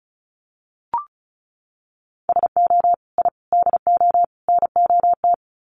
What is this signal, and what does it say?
Morse code, heard as a keyed tone, about 34 wpm, 700 Hz. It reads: SO I DO NOT
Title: Sets of 4 Words 35wpm